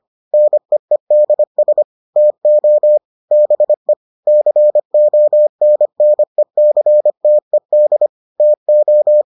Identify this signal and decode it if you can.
Morse code, heard as a keyed tone, about 25 wpm, 600 Hz. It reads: NEEDS TO BE CONNECTED TO